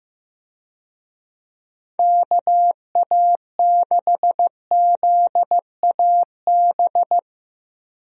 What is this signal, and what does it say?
Morse code, heard as a keyed tone, about 15 wpm, 700 Hz. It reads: KA6ZAB
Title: USA Call Signs - 6 Characters - Repeated 2x - 15wpm